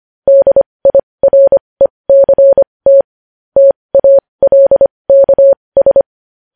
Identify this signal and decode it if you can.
Morse code, heard as a keyed tone, about 25 wpm, spaced as for 20 wpm, 550 Hz. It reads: DIRECT TALKS
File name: podcast / News headlines-1x-25x20